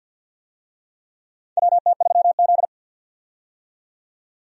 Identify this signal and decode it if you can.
Morse code, heard as a keyed tone, about 47 wpm, 700 Hz. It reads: WT3Z
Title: USA Call Signs - 4 Characters - 50wpm